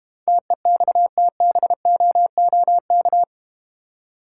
Morse code, heard as a keyed tone, about 31 wpm, 700 Hz. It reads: TEXTBOOK
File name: Easy Words - Speed-Racing with Hints - 25wpm